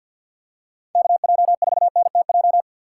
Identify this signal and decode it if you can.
Morse code, heard as a keyed tone, about 49 wpm, 690 Hz. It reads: KJ4NTJ
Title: USA Call Signs - 6x Character Spacing - 50wpm